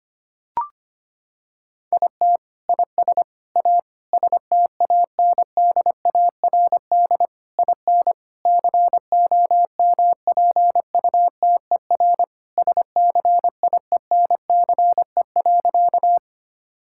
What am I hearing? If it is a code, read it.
Morse code, heard as a keyed tone, about 25 wpm, 700 Hz. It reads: IT IS A STANDARD IN COMPUTER SCIENCE.
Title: Sentences from Top 1000 Words - Review Part 6 - 25wpm